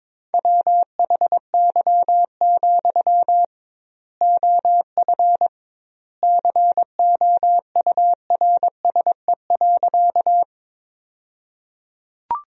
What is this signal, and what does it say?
Morse code, heard as a keyed tone, about 22 wpm, 700 Hz. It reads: WHY, OF COURSE.
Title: 2x Word-Spacing - Sentences from Top 500 Words - 22wpm